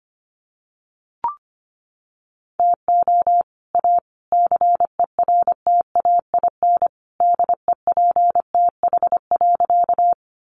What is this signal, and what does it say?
Morse code, heard as a keyed tone, about 25 wpm, 700 Hz. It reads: TO A CERTAIN DEPTH.
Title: Sentences from Top 1200 Words 25wpm